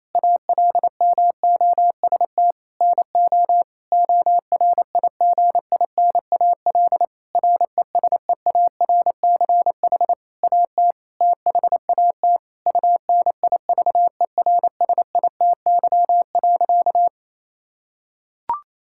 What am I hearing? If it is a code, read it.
Morse code, heard as a keyed tone, about 28 wpm, 700 Hz. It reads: ALMOST NO ORIGINAL RESEARCH AT THAT UNIVERSITY.
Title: Sentences from Top 1000 Words - Review Part 6 - 28wpm